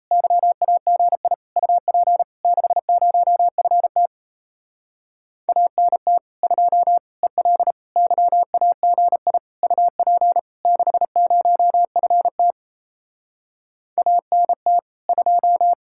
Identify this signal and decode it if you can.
Morse code, heard as a keyed tone, about 38 wpm, 700 Hz. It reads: YAGI UP 60FT ANT 2 EL YAGI UP 60FT ANT 2